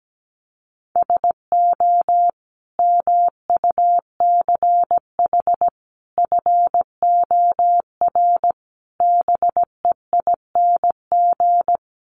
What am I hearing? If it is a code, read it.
Morse code, heard as a keyed tone, about 17 wpm, 700 Hz. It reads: SO MUCH FOR BEING
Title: Sets of 5 Words Encore 17wpm